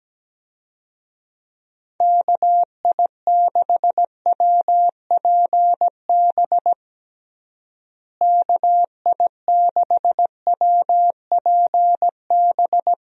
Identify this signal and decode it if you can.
Morse code, heard as a keyed tone, about 17 wpm, 700 Hz. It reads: KI6WPB KI6WPB
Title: USA Call Signs - 6 Characters - Repeated 3x - 17wpm